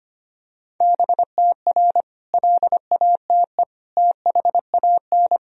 Morse code, heard as a keyed tone, about 25 wpm, 700 Hz. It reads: BTR LATE THAN
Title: Sentences from Top 400 Words and Abbreviations 25wpm